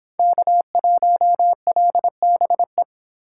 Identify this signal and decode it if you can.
Morse code, heard as a keyed tone, about 25 wpm, 700 Hz. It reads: K1LBE